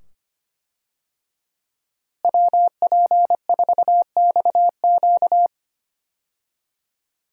Morse code, heard as a keyed tone, about 25 wpm, 700 Hz. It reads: WP4XQ
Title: USA Call Signs - 5 Characters - Repeated 2x - 25wpm